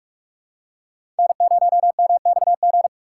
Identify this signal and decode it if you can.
Morse code, heard as a keyed tone, about 41 wpm, 700 Hz. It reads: N0MXG
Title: USA Call Signs - 4x Character Spacing - 45wpm